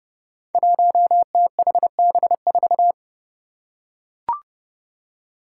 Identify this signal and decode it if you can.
Morse code, heard as a keyed tone, about 32 wpm, 700 Hz. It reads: 1THB4
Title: Five Challenging Letter-Number Combinations 30wpm